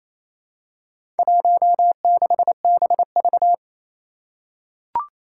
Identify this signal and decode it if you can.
Morse code, heard as a keyed tone, about 28 wpm, 700 Hz. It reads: 16BV
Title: Four Challenging Letter-Number Combinations 28wpm